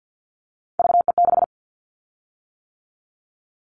Morse code, heard as a keyed tone, about 50 wpm, 690 Hz. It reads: VE6
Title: Callsign Prefixes for Top 100 DXCC - 50wpm